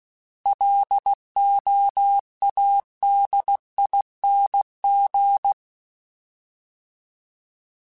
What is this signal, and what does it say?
Morse code, heard as a keyed tone, about 16 wpm, 800 Hz. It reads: LOADING
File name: Alphabetical / Alphabetical mp3s / Ls @15